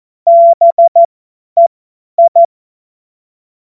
Morse code, heard as a keyed tone, about 14 wpm, spaced as for 10 wpm, 680 Hz. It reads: BEI
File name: Morsekurs / Witze / Witz 27. September 2022 50 F70